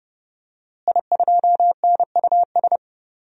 Morse code, heard as a keyed tone, about 30 wpm, 700 Hz. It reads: I2NUS
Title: DXCC-100 Call Signs - 5 Characters - Repeated 2x - 30wpm